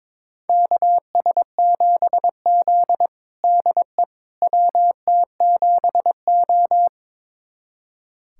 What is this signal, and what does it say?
Morse code, heard as a keyed tone, about 21 wpm, 700 Hz. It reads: KS7Z DE WT7O
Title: USA Callsign DE Callsign - 4 Characters - 22wpm